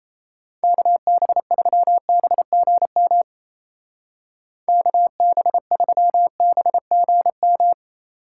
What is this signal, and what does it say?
Morse code, heard as a keyed tone, about 34 wpm, 700 Hz. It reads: KB3BGM KB3BGM
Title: Speed Racing - USA 2x3 Call Signs 28wpm